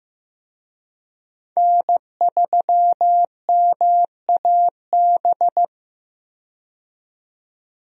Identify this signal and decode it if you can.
Morse code, heard as a keyed tone, about 15 wpm, 700 Hz. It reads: N3MAB
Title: USA Call Signs - 5 Characters - 15wpm